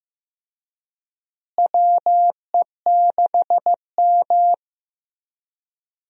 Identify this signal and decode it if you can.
Morse code, heard as a keyed tone, about 15 wpm, 700 Hz. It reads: WE6M
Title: USA Call Signs - 4 Characters - Repeated 3x - 15wpm